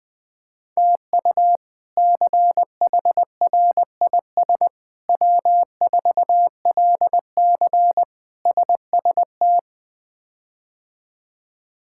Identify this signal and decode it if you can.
Morse code, heard as a keyed tone, about 20 wpm, 700 Hz. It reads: TU CHRIS W4LC SST
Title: K1USN SST QSO Practice - 20wpm